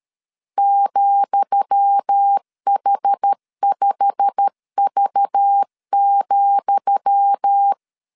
Morse code, heard as a keyed tone, about 13 wpm, 800 Hz. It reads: ,H5V,
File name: LEZIONI / 10 SIGNS / 10D COMA 12wpm 2544